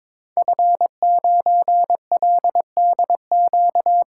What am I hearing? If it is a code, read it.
Morse code, heard as a keyed tone, about 22 wpm, 700 Hz. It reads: F9LDQ
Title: Code Groups - 5 Letter:Numbers - 22wpm